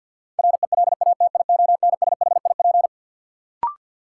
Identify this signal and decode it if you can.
Morse code, heard as a keyed tone, about 52 wpm, 700 Hz. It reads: RELATIONSHIP
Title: Sentences from Top 1100 Words - New Word Drill Alt - 50wpm